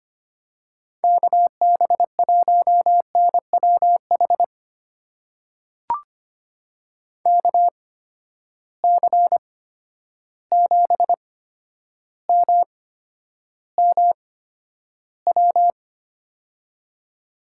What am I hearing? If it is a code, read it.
Morse code, heard as a keyed tone, about 25 wpm, spaced as for 11 wpm, 700 Hz. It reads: KB1NWH KC7MMW
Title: USA Call Signs - 6 Characters - 8x Character Spacing - 25wpm